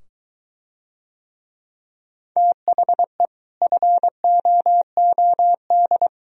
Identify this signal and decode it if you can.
Morse code, heard as a keyed tone, about 23 wpm, 700 Hz. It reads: THE FOOD